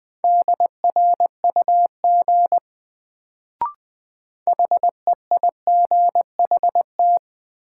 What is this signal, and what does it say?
Morse code, heard as a keyed tone, about 20 wpm, 700 Hz. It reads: DRUG HEIGHT
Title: Sets of 5 Words - New Word Drill - 20wpm